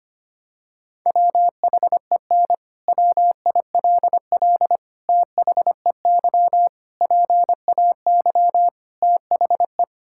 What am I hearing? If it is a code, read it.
Morse code, heard as a keyed tone, about 25 wpm, 700 Hz. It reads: WHEN WILL THEY PAY THE
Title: Sentences from Top 1000 Words - Review Part 5 - 25wpm